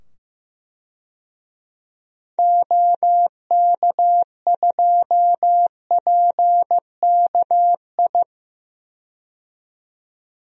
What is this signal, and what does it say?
Morse code, heard as a keyed tone, about 15 wpm, 700 Hz. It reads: OK2PKI
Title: DXCC-100 Call Signs - 6 Characters - 15wpm